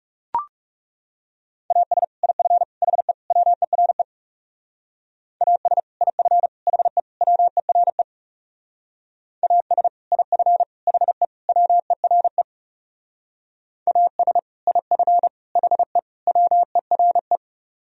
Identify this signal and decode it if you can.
Morse code, heard as a keyed tone, about 39 wpm, 700 Hz. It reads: AS IF HE WERE AS IF HE WERE AS IF HE WERE AS IF HE WERE